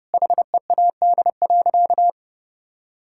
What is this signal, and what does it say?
Morse code, heard as a keyed tone, about 32 wpm, 700 Hz. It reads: HEAD.
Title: Sentences from Top 1600 Words 30wpm